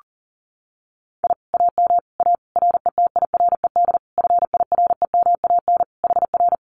Morse code, heard as a keyed tone, about 40 wpm, 700 Hz. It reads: I AM A RETIRED FIREMAN HR
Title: Common QSO Phrases Encore 40wpm